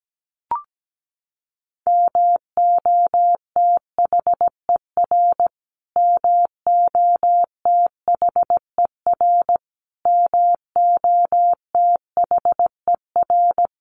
Morse code, heard as a keyed tone, about 17 wpm, 700 Hz. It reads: MOTHER MOTHER MOTHER
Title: Top 500 Words Repeated 17wpm